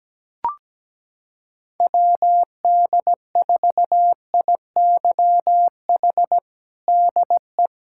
Morse code, heard as a keyed tone, about 17 wpm, 700 Hz. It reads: WD4IYH DE
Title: Practice QSO - Exchange 4 - Ending - 17wpm